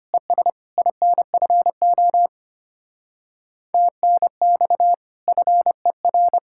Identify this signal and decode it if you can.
Morse code, heard as a keyed tone, about 31 wpm, 700 Hz. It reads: ES INFO TNX FER